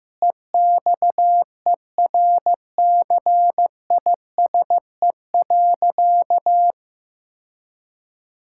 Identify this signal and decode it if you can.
Morse code, heard as a keyed tone, about 15 wpm, 700 Hz. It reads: EXERCISE.
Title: Sentences from Top 1000 Words - Review Part 6 - 15wpm